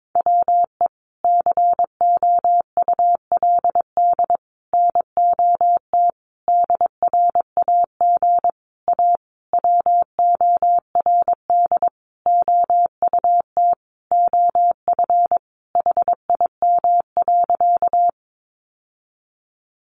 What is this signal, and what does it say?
Morse code, heard as a keyed tone, about 22 wpm, 700 Hz. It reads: WE COULD NOT DRAG A WORD OUT OF HIM.
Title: Sentences from Top 1500 Words 22wpm